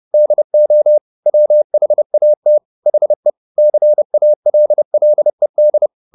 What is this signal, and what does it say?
Morse code, heard as a keyed tone, about 30 wpm, 600 Hz. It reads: DO WHAT HE CALLED